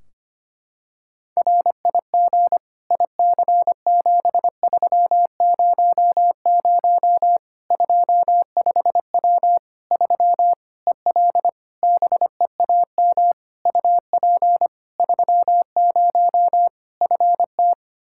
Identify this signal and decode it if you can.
Morse code, heard as a keyed tone, about 25 wpm, 700 Hz. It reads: RIG IC7300 25W 3 EL BEAM UP 30 FT
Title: Practice QSO - Exchange 2 - Info - 25wpm